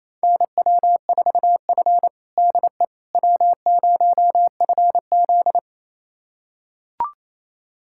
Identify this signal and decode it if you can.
Morse code, heard as a keyed tone, about 28 wpm, 700 Hz. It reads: NW4F DE W0FZ
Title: USA Callsign DE Callsign - 4 Characters - 28wpm